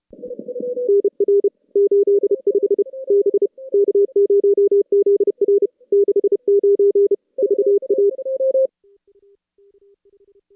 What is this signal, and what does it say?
Morse code, heard as a keyed tone, about 31 wpm, 410 Hz. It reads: NR 85 B K0ZR 69 VA
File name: contests / CW-SS-k5tr- -2200